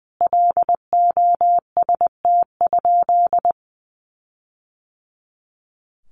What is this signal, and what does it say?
Morse code, heard as a keyed tone, about 20 wpm, 700 Hz. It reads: LOST?
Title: Sentences from Top 1400 Words 20wpm